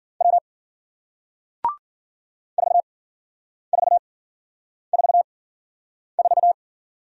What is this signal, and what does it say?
Morse code, heard as a keyed tone, about 51 wpm, spaced as for 9 wpm, 700 Hz. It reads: W 4444